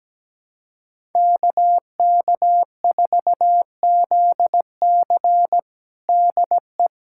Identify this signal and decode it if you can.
Morse code, heard as a keyed tone, about 17 wpm, 700 Hz. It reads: KK4ZC DE
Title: USA Callsign DE Callsign - 5 Characters - 17wpm